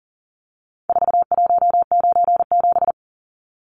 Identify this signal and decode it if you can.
Morse code, heard as a keyed tone, about 40 wpm, 700 Hz. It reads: 4197
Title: Numbers - 4-Digit 40wpm